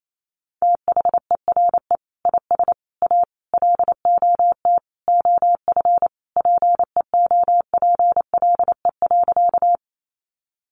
Morse code, heard as a keyed tone, about 28 wpm, 700 Hz. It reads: THERE IS A LOT OF PEOPLE.
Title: Sentences from Top 500 Words 28wpm